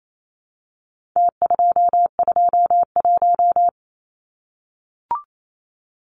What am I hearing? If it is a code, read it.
Morse code, heard as a keyed tone, about 28 wpm, 700 Hz. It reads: T221
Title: Numbers - Contest Style - 4-Digit 28wpm